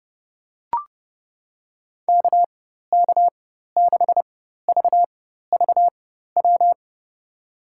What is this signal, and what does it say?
Morse code, heard as a keyed tone, about 30 wpm, spaced as for 14 wpm, 700 Hz. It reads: KK6VVW
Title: USA Call Signs - 6 Characters - 4x Character Spacing - 30wpm